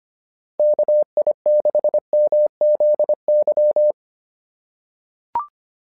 Morse code, heard as a keyed tone, about 25 wpm, 600 Hz. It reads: KI6MZY
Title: SOTA QSO Practice 25wpm